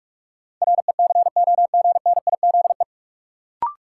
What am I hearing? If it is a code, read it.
Morse code, heard as a keyed tone, about 46 wpm, 700 Hz. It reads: RECOGNIZE
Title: Sentences from Top 1400 Words - New Word Drill - 45wpm